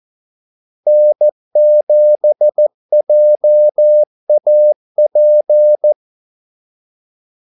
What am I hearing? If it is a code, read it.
Morse code, heard as a keyed tone, about 14 wpm, 600 Hz. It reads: N7JAP
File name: Callsigns-14wpm